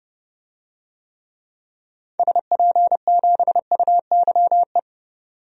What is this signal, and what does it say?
Morse code, heard as a keyed tone, about 30 wpm, 700 Hz. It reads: SP7UYE
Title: DXCC-100 Call Signs - 6 Characters - 30wpm